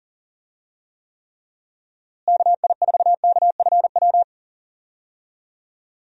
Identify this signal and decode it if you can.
Morse code, heard as a keyed tone, about 40 wpm, 700 Hz. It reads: KI4KFW
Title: USA Call Signs - 6 Characters - Repeated 3x - 40wpm